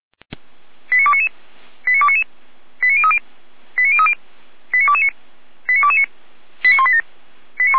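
Five-tone selective call (ZVEI), heard as quick runs of five tones, tones of about 70 ms. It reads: 89299 89299 89030 89030 89229 89229 89187
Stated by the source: Five-tone ZVEI selective call received by SDR, from the feuerwehr-orlamuende.org fire brigade audio archive.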